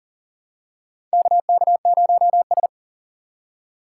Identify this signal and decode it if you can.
Morse code, heard as a keyed tone, about 40 wpm, 700 Hz. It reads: KK0S
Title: USA Call Signs - 4 Characters - 2x Character Spacing - 40wpm